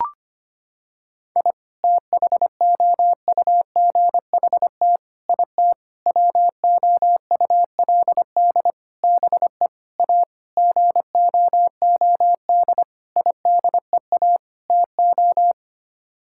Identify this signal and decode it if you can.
Morse code, heard as a keyed tone, about 25 wpm, 700 Hz. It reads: I THOUGHT IT WOULD BE A GOOD IDEA TO
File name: Sets of 9 Words - 25wpm